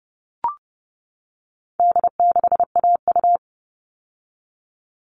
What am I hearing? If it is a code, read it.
Morse code, heard as a keyed tone, about 30 wpm, 700 Hz. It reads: D6AU
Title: All Cut Numbers - 4-Digit 30wpm